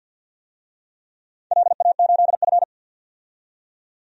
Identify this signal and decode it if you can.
Morse code, heard as a keyed tone, about 48 wpm, 700 Hz. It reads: LA8F
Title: DXCC-100 Call Signs - 4 Characters - 50wpm